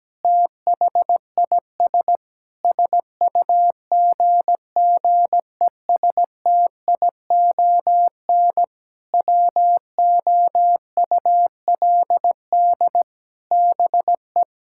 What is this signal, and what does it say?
Morse code, heard as a keyed tone, about 17 wpm, 700 Hz. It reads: THIS SUGGESTION WOULD BE
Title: Sentences from Top 1600 Words 17wpm